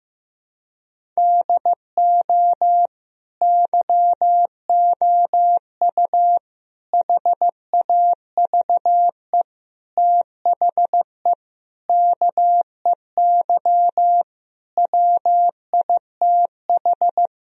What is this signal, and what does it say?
Morse code, heard as a keyed tone, about 15 wpm, 700 Hz. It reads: DO YOU HAVE THE KEY WITH
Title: Sentences from Top 1000 Words - Review Part 4 - 15wpm